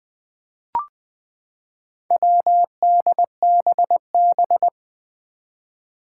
Morse code, heard as a keyed tone, about 20 wpm, 700 Hz. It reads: WDBB
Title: Four Challenging Letter-Number Combinations 20wpm